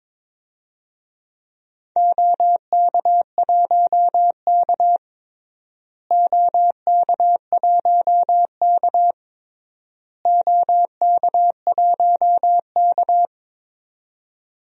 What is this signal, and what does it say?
Morse code, heard as a keyed tone, about 22 wpm, 700 Hz. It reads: OK1K OK1K OK1K
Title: DXCC-100 Call Signs - 4 Characters - Repeated 3x - 22wpm